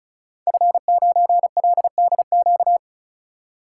Morse code, heard as a keyed tone, about 36 wpm, 700 Hz. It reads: F9LDQ
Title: Code Groups - 5 Letter:Numbers - 35wpm